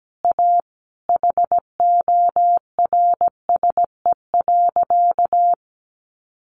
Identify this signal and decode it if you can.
Morse code, heard as a keyed tone, about 17 wpm, 700 Hz. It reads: A HORSE.
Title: Sentences from Top 1100 Words 17wpm